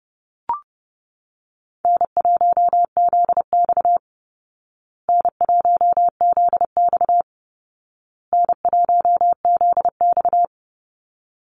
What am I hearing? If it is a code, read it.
Morse code, heard as a keyed tone, about 30 wpm, 700 Hz. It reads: N1ZX N1ZX N1ZX
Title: Easy Call Signs Repeated 3x - 30wpm